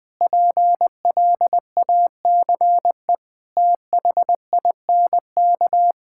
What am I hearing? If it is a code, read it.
Morse code, heard as a keyed tone, about 20 wpm, 700 Hz. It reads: PLACE THINK
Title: Sentences with 3-Word Phrases - 20wpm